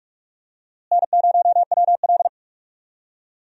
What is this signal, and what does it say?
Morse code, heard as a keyed tone, about 46 wpm, 700 Hz. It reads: N0WL
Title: USA Call Signs - 4 Characters - Repeated 2x - 45wpm